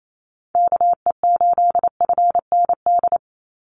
Morse code, heard as a keyed tone, about 28 wpm, 700 Hz. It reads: KE8FND
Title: USA Call Signs 28wpm